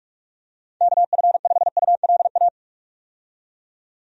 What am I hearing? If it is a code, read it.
Morse code, heard as a keyed tone, about 46 wpm, 700 Hz. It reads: KF5ULA
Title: USA Call Signs - 6 Characters - 45wpm